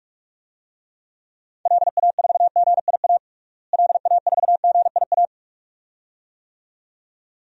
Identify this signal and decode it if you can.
Morse code, heard as a keyed tone, about 43 wpm, 700 Hz. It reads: LA4GIA LA4GIA
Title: DXCC-100 Call Signs - 6 Characters - Repeated 2x - 45wpm